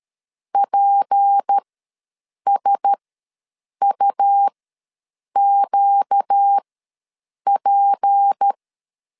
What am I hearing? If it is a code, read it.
Morse code, heard as a keyed tone, about 13 wpm, spaced as for 7 wpm, 800 Hz. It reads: PSUQP
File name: LEZIONI / 6 PKX0 / 6 P ws 12wpm 3117